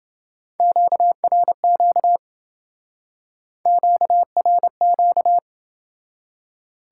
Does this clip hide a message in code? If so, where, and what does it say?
Yes: Morse code from 0.6 to 5.4 s, heard as a keyed tone, about 29 wpm, 700 Hz. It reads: QRQ QRQ